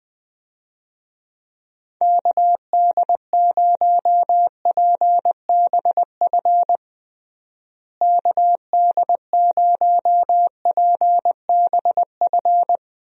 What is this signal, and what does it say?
Morse code, heard as a keyed tone, about 20 wpm, 700 Hz. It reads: KD0PBF KD0PBF
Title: USA Call Signs - 6 Characters - Repeated 3x - 20wpm